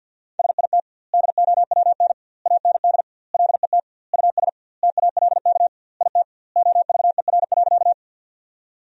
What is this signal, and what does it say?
Morse code, heard as a keyed tone, about 50 wpm, 700 Hz. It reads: SIT DOWN AND LET US TALK IT OVER.
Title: 2x Word-Spacing - Sentences from Top 500 Words - 50wpm